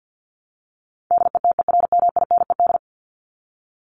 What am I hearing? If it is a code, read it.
Morse code, heard as a keyed tone, about 50 wpm, 700 Hz. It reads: DETERMINED
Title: ED Suffix - Top 100 Words with Base Words 50wpm